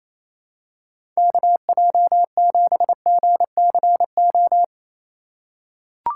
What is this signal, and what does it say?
Morse code, heard as a keyed tone, about 28 wpm, 700 Hz. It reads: KJ7GCO
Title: USA Call Signs - 6x Character Spacing - 28wpm